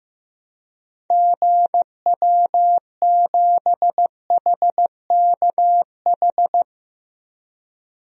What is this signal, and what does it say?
Morse code, heard as a keyed tone, about 15 wpm, 700 Hz. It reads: GW7HKH
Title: DXCC-100 Call Signs - 6 Characters - 15wpm